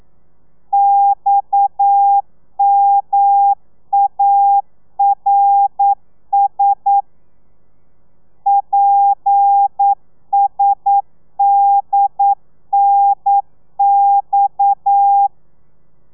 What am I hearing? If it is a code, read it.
Morse code, heard as a keyed tone, about 9 wpm, 800 Hz. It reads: XMARS PSDNX
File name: telegrafia / i2tfj / Lez7 40